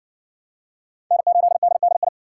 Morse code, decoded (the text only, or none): N8DDI